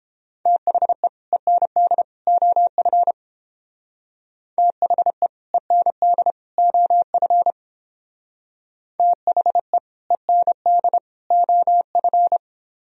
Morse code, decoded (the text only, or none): THE END OF THE END OF THE END OF